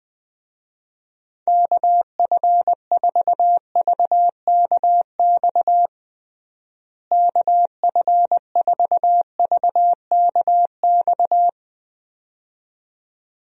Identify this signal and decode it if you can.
Morse code, heard as a keyed tone, about 20 wpm, 700 Hz. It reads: KF4VKX KF4VKX
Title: USA Call Signs - 6 Characters - Repeated 2x - 20wpm